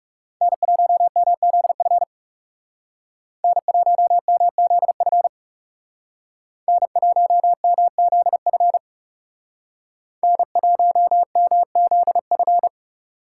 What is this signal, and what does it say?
Morse code, heard as a keyed tone, about 36 wpm, 700 Hz. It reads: N1MZF N1MZF N1MZF N1MZF
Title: Speed Racing - USA 1x3 Call Signs 30wpm